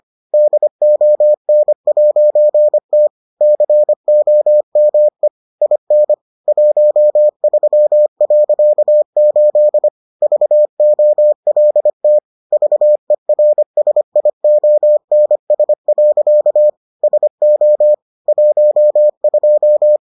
Morse code, heard as a keyed tone, about 25 wpm, 600 Hz. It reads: DON'T COME IN 13.8 VOLT VERSIONS. SO 12